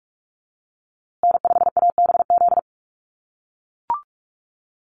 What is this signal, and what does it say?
Morse code, heard as a keyed tone, about 44 wpm, 700 Hz. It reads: N5ABZ
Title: USA Call Signs Repeated 2x 45wpm